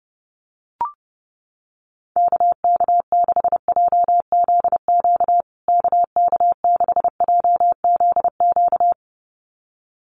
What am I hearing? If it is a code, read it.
Morse code, heard as a keyed tone, about 30 wpm, 700 Hz. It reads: KK6JZQ KK6JZQ
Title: USA Call Signs Repeated 2x 30wpm